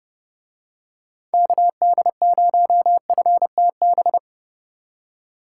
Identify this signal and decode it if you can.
Morse code, heard as a keyed tone, about 30 wpm, 700 Hz. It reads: KD0FTB